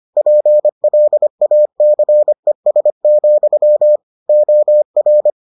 Morse code, heard as a keyed tone, about 25 wpm, 600 Hz. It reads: PLACES, OR